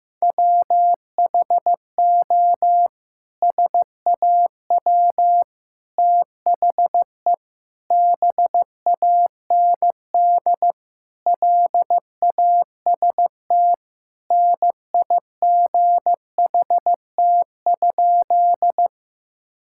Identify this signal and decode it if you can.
Morse code, heard as a keyed tone, about 15 wpm, 700 Hz. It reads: WHO SAW THE BAND LAST NIGHT?
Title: Sentences from Top 1600 Words 15wpm